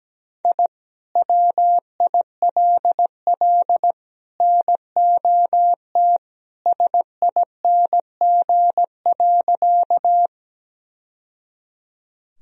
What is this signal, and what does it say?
Morse code, heard as a keyed tone, about 17 wpm, 700 Hz. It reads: I WILL NOT SING.
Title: Sentences from Top 1000 Words - Review Part 2 - 17wpm